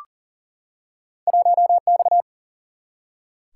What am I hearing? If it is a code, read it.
Morse code, heard as a keyed tone, about 39 wpm, 700 Hz. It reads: 1X
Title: Single-Letter Single-Number 40wpm